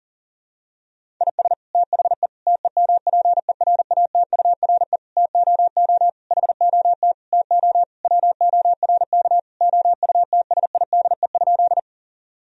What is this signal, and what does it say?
Morse code, heard as a keyed tone, about 40 wpm, 700 Hz. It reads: IS THE TEMPERATURE TOO HOT TO WORK OUTSIDE?
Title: Sentences from Top 1000 Words - Review Part 4 - 40wpm